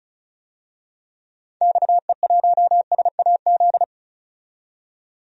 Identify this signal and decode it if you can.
Morse code, heard as a keyed tone, about 36 wpm, 700 Hz. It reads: XE1SAZ
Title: DXCC-100 Call Signs - 6 Characters - 35wpm